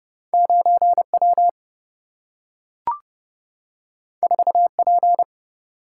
Morse code, heard as a keyed tone, about 30 wpm, 700 Hz. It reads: 9W 4P
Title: Single-Letter Single-Number 30wpm